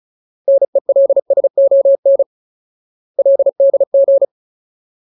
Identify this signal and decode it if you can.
Morse code, heard as a keyed tone, about 35 wpm, 550 Hz. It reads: NELSON LDG
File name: Ultimate Word List / 35WPM Ultimate